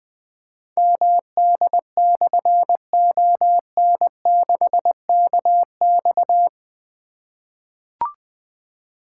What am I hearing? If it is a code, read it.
Morse code, heard as a keyed tone, about 20 wpm, 700 Hz. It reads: MD/ON6KX
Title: Difficult Call Signs 20wpm